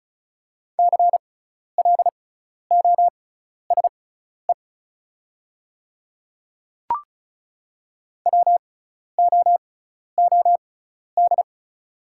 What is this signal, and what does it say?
Morse code, heard as a keyed tone, about 36 wpm, spaced as for 12 wpm, 690 Hz. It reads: CLOSE WOOD
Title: 6x Character Spacing - Top 300 Words - 35wpm